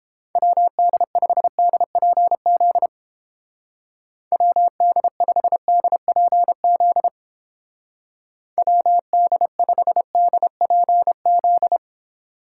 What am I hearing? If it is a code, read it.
Morse code, heard as a keyed tone, about 31 wpm, 700 Hz. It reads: WD5DPZ WD5DPZ WD5DPZ